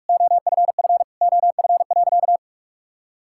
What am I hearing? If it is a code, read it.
Morse code, heard as a keyed tone, about 46 wpm, 700 Hz. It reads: OFF OF.
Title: Sentences from Top 1000 Words - Review Part 5 - 45wpm